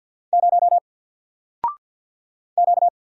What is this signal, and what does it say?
Morse code, heard as a keyed tone, about 48 wpm, 700 Hz. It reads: 0 Q
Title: Single Character (Letter:Number:Punctuation:Prosign) Alt 50wpm